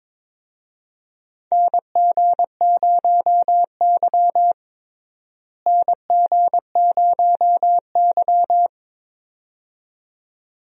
Morse code, heard as a keyed tone, about 22 wpm, 700 Hz. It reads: NG0Y NG0Y
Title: USA Call Signs - 4 Characters - Repeated 2x - 22wpm